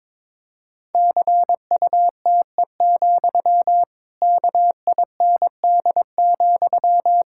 Morse code, heard as a keyed tone, about 22 wpm, 700 Hz. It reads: CUTE, KIND,
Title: Pangram Challenge v2 - 22wpm